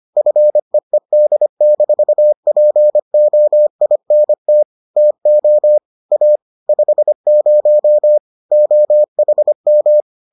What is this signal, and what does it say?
Morse code, heard as a keyed tone, about 25 wpm, 600 Hz. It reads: FEED-POINT TO A 50 OHM